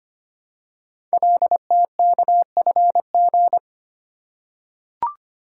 Morse code, heard as a keyed tone, about 25 wpm, 700 Hz. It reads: LTKFG
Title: Code Groups - 5 Letters - 25wpm